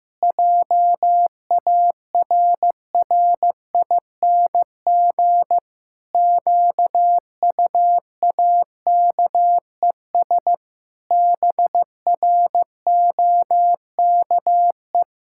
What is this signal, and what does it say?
Morse code, heard as a keyed tone, about 15 wpm, 700 Hz. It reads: JARRING QUAKES BROKE
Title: Pangram Challenge v2 - 15wpm